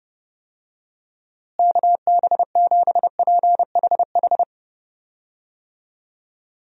Morse code, heard as a keyed tone, about 30 wpm, 700 Hz. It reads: KB7PHH
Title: USA Call Signs - 6 Characters - Repeated 3x - 30wpm